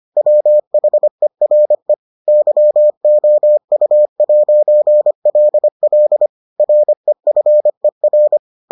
Morse code, heard as a keyed tone, about 25 wpm, 600 Hz. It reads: WHERE YOU'LL REFER